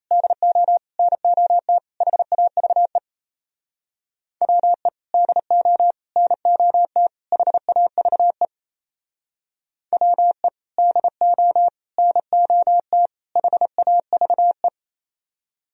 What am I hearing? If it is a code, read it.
Morse code, heard as a keyed tone, about 35 wpm, 700 Hz. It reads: DO NOT HAVE WE DO NOT HAVE WE DO NOT HAVE